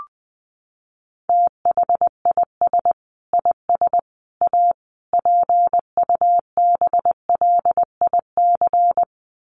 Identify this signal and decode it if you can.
Morse code, heard as a keyed tone, about 20 wpm, 700 Hz. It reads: THIS IS A PUBLIC
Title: Sentences from Top 600 Words 20wpm